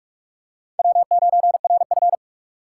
Morse code, heard as a keyed tone, about 41 wpm, 700 Hz. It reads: W9RF
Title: USA Call Signs - 4 Characters - 2x Character Spacing - 45wpm